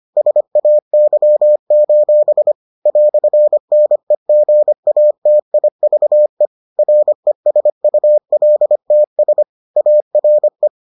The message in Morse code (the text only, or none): SAY: "NEGATIVE RESULTS ARE